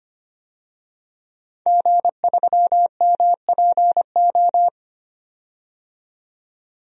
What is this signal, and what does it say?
Morse code, heard as a keyed tone, about 25 wpm, 700 Hz. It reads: G3MPO
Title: DXCC-100 Call Signs - 5 Characters - 25wpm